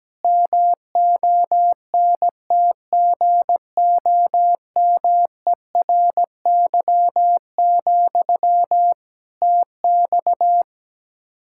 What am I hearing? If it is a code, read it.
Morse code, heard as a keyed tone, about 17 wpm, 700 Hz. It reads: MONTGOMERY, TX
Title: Practice QSO - Exchange 1 - Beginning - 17wpm